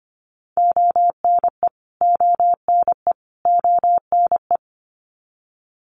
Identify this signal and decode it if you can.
Morse code, heard as a keyed tone, about 25 wpm, 700 Hz. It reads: ONE ONE ONE
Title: Top 200 Words Repeated 25wpm